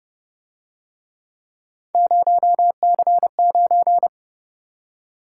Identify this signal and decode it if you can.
Morse code, heard as a keyed tone, about 28 wpm, 700 Hz. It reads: 0C9